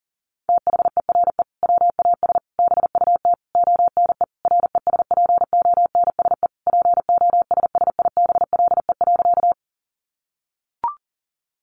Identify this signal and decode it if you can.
Morse code, heard as a keyed tone, about 40 wpm, 700 Hz. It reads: THERE WAS BUT ONE RESPONSE POSSIBLE.
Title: Sentences from Top 1400 Words 40wpm